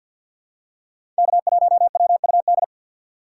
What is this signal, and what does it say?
Morse code, heard as a keyed tone, about 53 wpm, 700 Hz. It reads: K1WUD
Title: USA Call Signs - 5 Characters - 3x Character Spacing - 50wpm